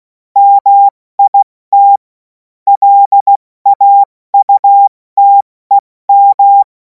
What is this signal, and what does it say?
Morse code, heard as a keyed tone, about 16 wpm, spaced as for 14 wpm, 800 Hz. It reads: MIT LAUTEM
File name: Morsekurs / Witze / Witz 17. Februar 2026 70 F80